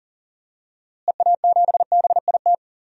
Morse code, heard as a keyed tone, about 39 wpm, 700 Hz. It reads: EA7BIT
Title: DXCC-100 Call Signs - 6 Characters - Repeated 2x - 40wpm